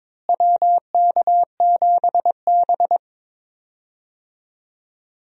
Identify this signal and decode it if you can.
Morse code, heard as a keyed tone, about 22 wpm, 700 Hz. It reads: WK7B
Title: USA Call Signs - 4 Characters - Repeated 3x - 22wpm